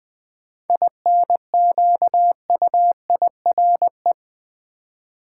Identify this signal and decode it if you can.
Morse code, heard as a keyed tone, about 20 wpm, 700 Hz. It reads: INQUIRE
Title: 7 Letter Words - First 3 Letters Given - 20wpm